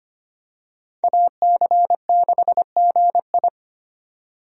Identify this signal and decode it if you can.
Morse code, heard as a keyed tone, about 25 wpm, 700 Hz. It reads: AC6GI
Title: USA Call Signs - 5 Characters - 25wpm